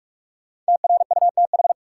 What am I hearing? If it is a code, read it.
Morse code, heard as a keyed tone, about 47 wpm, 700 Hz. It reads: TRUTH
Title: Sentences from Top 800 Words - New Word Drill - 45wpm